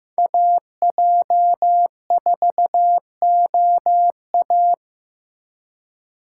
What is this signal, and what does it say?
Morse code, heard as a keyed tone, about 15 wpm, 700 Hz. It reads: AJ4OA
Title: USA Call Signs - 4x Character Spacing - 15wpm